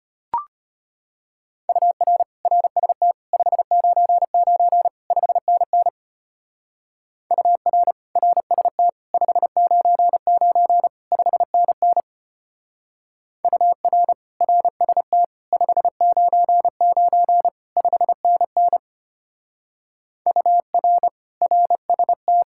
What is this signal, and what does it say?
Morse code, heard as a keyed tone, about 31 wpm, 700 Hz. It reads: UR RST 599 5NN UR RST 599 5NN UR RST 599 5NN UR RST
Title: Speed-Racing Common QSO Phrases - 25wpm